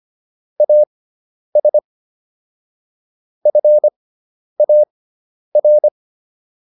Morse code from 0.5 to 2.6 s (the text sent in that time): AS